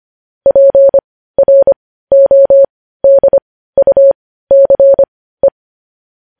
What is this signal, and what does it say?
Morse code, heard as a keyed tone, about 25 wpm, spaced as for 15 wpm, 550 Hz. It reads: PRODUCE